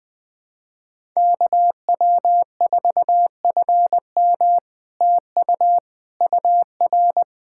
Morse code, heard as a keyed tone, about 20 wpm, 700 Hz. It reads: KW4FM TU UR
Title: POTA QSO Practice - 20wpm